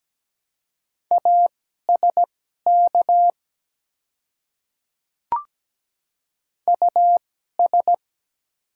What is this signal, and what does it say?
Morse code, heard as a keyed tone, about 17 wpm, spaced as for 12 wpm, 690 Hz. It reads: ASK US
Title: 2x Character Spacing - Top 300 Words - 17wpm